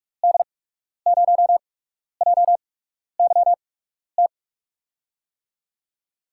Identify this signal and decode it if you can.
Morse code, heard as a keyed tone, about 47 wpm, spaced as for 12 wpm, 700 Hz. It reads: D0JYT
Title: USA Call Signs - 6 Characters - 8x Character Spacing - 45wpm